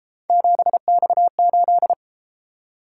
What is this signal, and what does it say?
Morse code, heard as a keyed tone, about 28 wpm, 700 Hz. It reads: 7X8